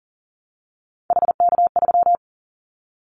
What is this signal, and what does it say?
Morse code, heard as a keyed tone, about 43 wpm, 700 Hz. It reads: HK3
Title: Callsign Prefixes for Top 100 DXCC - 40wpm